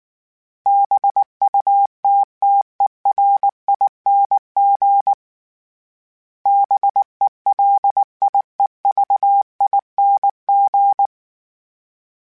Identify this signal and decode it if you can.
Morse code, heard as a keyed tone, about 19 wpm, 800 Hz. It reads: BUTTERING BELIEVING